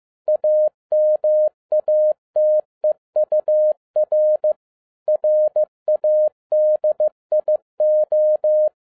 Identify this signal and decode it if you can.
Morse code, heard as a keyed tone, about 15 wpm, 610 Hz. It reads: AMATEUR RADIO
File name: 2011-07-mp3 / n07 A LOT HAS CHANGED 15wpm